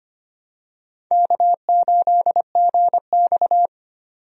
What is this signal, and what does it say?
Morse code, heard as a keyed tone, about 25 wpm, 700 Hz. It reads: K8GX
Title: USA Call Signs - 4 Characters - 25wpm